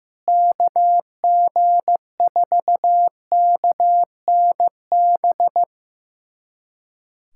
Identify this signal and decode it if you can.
Morse code, heard as a keyed tone, about 15 wpm, 700 Hz. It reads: KG4KNB
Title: USA Callsign DE Callsign - 6 Characters - 15wpm